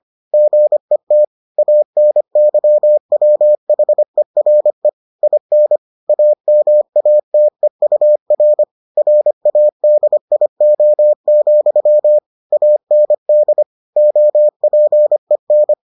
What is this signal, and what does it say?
Morse code, heard as a keyed tone, about 25 wpm, 600 Hz. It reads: GET ANYWHERE IN AMATEUR RADIO, AND OPEN